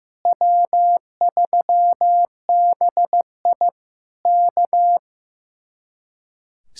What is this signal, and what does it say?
Morse code, heard as a keyed tone, about 15 wpm, 700 Hz. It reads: W3BI K
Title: Speed-Racing Common QSO Phrases Encore - 15wpm